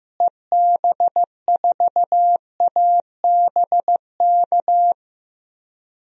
Morse code, heard as a keyed tone, about 15 wpm, 700 Hz. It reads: EB4ABK
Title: DXCC-100 Call Signs - 6 Characters - Repeated 3x - 15wpm